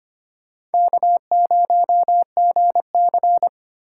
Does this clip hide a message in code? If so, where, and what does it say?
Yes: Morse code from 0.7 to 3.5 s, heard as a keyed tone, about 33 wpm, 700 Hz. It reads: K0GC